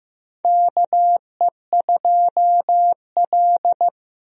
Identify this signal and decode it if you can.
Morse code, heard as a keyed tone, about 15 wpm, 700 Hz. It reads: KE2L
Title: USA Call Signs - 4 Characters - Repeated 3x - 15wpm